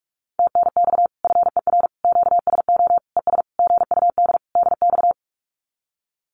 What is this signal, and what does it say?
Morse code, heard as a keyed tone, about 46 wpm, 700 Hz. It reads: TNX FER QSO ES GUD DX
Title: Common QSO Phrases Encore 45wpm